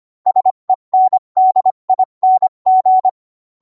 Morse code, heard as a keyed tone, about 25 wpm, 750 Hz. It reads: SENDING